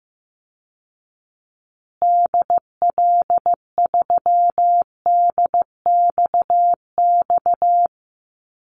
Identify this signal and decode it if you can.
Morse code, heard as a keyed tone, about 15 wpm, 700 Hz. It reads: DL3DXX
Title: Call Signs Repeated 3x - 15wpm